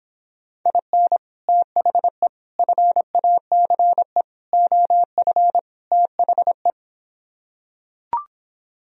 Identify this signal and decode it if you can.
Morse code, heard as a keyed tone, about 26 wpm, 700 Hz. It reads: IN THE FACE OF THE